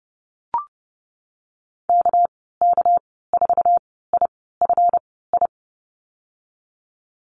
Morse code, heard as a keyed tone, about 30 wpm, spaced as for 17 wpm, 700 Hz. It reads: KK4IFI
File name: Lesson 37g - Letter x - Call Signs 30wpm